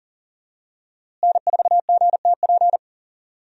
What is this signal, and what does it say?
Morse code, heard as a keyed tone, about 38 wpm, 700 Hz. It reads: N4GTP
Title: USA Call Signs - 5 Characters - Repeated 2x - 40wpm